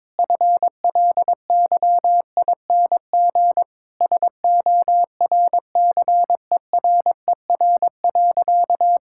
Morse code, heard as a keyed tone, about 22 wpm, 690 Hz. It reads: FLYING SORCERER.
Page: morse-code-ninja-practice-22wpm